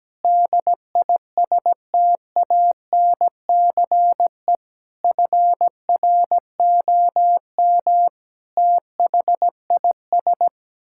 Morse code, heard as a keyed tone, about 17 wpm, 700 Hz. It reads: DISTANCE FROM THIS
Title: Sentences from Top 1600 Words 17wpm